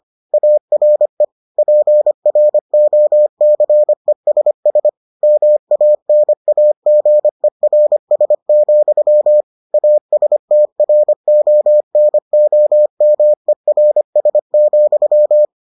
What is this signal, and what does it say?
Morse code, heard as a keyed tone, about 25 wpm, 600 Hz. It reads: ARE PROCESS MANAGERS, ASTRONOMERS,